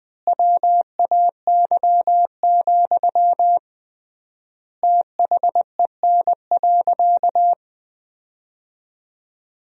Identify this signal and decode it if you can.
Morse code, heard as a keyed tone, about 21 wpm, 700 Hz. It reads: WAY, THEN.
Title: 3x Word-Spacing - Sentences from Top 400 Words - 20wpm